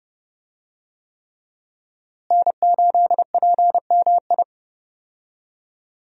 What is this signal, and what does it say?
Morse code, heard as a keyed tone, about 32 wpm, 700 Hz. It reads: N8PMI
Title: USA Call Signs - 5 Characters - Repeated 3x - 30wpm